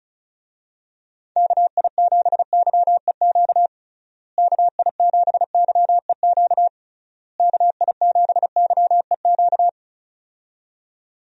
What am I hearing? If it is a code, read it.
Morse code, heard as a keyed tone, about 34 wpm, 700 Hz. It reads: KI7YEQ KI7YEQ KI7YEQ
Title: USA Call Signs - 6 Characters - Repeated 3x - 35wpm